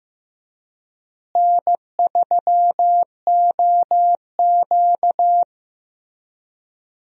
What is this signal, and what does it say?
Morse code, heard as a keyed tone, about 15 wpm, 700 Hz. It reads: N3OQ
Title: USA Call Signs - 4 Characters - Repeated 2x - 15wpm